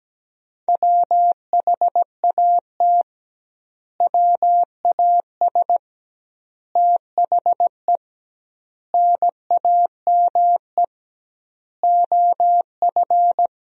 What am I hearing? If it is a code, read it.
Morse code, heard as a keyed tone, about 17 wpm, 700 Hz. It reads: WHAT WAS THE NAME OF
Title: 2x Word-Spacing - Sentences from Top 300 Words - 17wpm